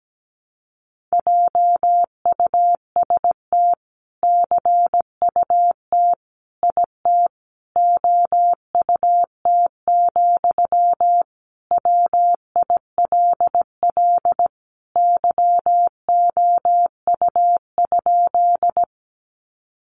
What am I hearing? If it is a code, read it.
Morse code, heard as a keyed tone, about 17 wpm, 700 Hz. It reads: JUST CUT IT OUT, WILL YOU?
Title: Sentences from Top 300 Words 17wpm